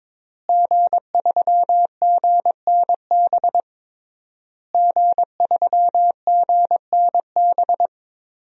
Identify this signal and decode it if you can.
Morse code, heard as a keyed tone, about 22 wpm, 700 Hz. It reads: G3GNB G3GNB
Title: DXCC-100 Call Signs - 5 Characters - Repeated 3x - 22wpm